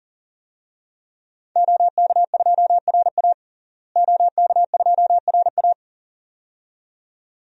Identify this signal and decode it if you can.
Morse code, heard as a keyed tone, about 40 wpm, 700 Hz. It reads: OK2RA OK2RA
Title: DXCC-100 Call Signs - 5 Characters - Repeated 2x - 40wpm